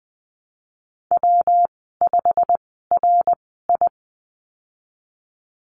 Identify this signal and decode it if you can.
Morse code, heard as a keyed tone, about 20 wpm, spaced as for 15 wpm, 700 Hz. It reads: W5RI
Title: 167 Lesson 27g - Number 7 - Call Signs 20wpm